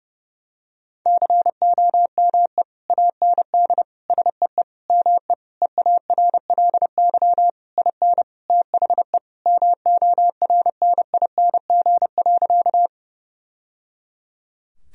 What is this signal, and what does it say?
Morse code, heard as a keyed tone, about 32 wpm, 700 Hz. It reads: COME AND SEE ME EARLY IN THE MORNING.
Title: Sentences from Top 1000 Words - Review Part 2 - 30wpm